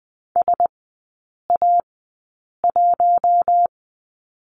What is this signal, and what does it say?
Morse code, heard as a keyed tone, about 20 wpm, spaced as for 8 wpm, 700 Hz. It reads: SA1
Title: 067 Lesson 12e - Letter d - Three Characters 20wpm